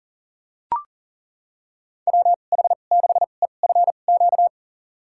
Morse code, heard as a keyed tone, about 40 wpm, spaced as for 29 wpm, 700 Hz. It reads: WH6EFQ
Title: USA Call Signs - 2x Character Spacing - 40wpm